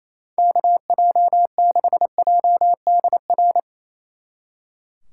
Morse code, heard as a keyed tone, about 36 wpm, 700 Hz. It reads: KJ6JDR